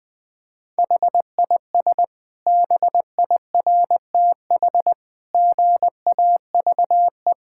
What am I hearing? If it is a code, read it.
Morse code, heard as a keyed tone, about 19 wpm, 700 Hz. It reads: HIS BIRTH GAVE
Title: Sentences from Top 1000 Words - Review Part 5 - 20wpm